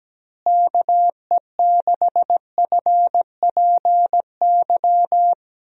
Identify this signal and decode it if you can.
Morse code, heard as a keyed tone, about 17 wpm, 700 Hz. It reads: KE6FPY
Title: USA Call Signs - 6 Characters - Repeated 3x - 17wpm